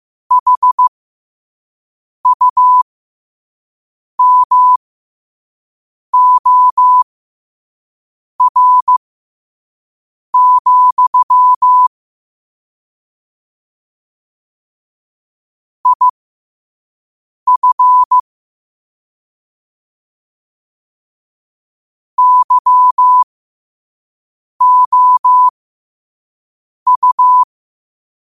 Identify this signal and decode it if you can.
Morse code, heard as a keyed tone, about 15 wpm, spaced as for 5 wpm, 1000 Hz. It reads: HUMOR, IF YOU
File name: qotdfiles / 5wpm